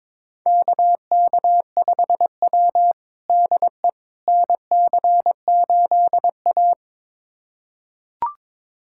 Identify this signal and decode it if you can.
Morse code, heard as a keyed tone, about 22 wpm, 700 Hz. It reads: KK5W DE NC8A
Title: USA Callsign DE Callsign - 4 Characters - 22wpm